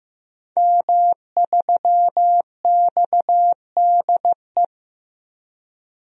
Morse code, heard as a keyed tone, about 15 wpm, 700 Hz. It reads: M3XDE
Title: DXCC-100 Call Signs - 5 Characters - Repeated 2x - 15wpm